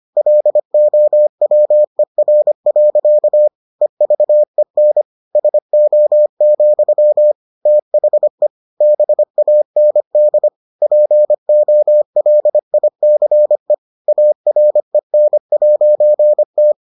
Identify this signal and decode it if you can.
Morse code, heard as a keyed tone, about 25 wpm, 600 Hz. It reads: LOWER. EVEN SO, THE BAND POLICE AREN'T